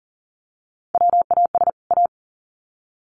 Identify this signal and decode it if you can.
Morse code, heard as a keyed tone, about 40 wpm, 700 Hz. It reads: WAS A
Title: Sets of 2 Words 40wpm